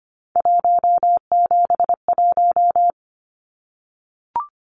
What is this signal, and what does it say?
Morse code, heard as a keyed tone, about 25 wpm, 700 Hz. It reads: 171
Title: Numbers - 3-Digit 25wpm